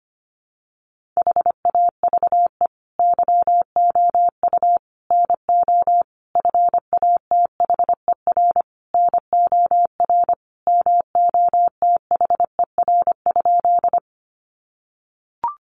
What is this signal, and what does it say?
Morse code, heard as a keyed tone, about 25 wpm, 700 Hz. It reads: HAVE YOU NO FATHER NOR MOTHER?
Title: Sentences from Top 1500 Words 25wpm